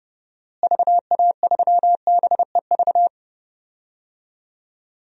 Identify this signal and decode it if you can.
Morse code, heard as a keyed tone, about 32 wpm, 700 Hz. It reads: VA3BEV
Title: DXCC-100 Call Signs - 6 Characters - Repeated 3x - 30wpm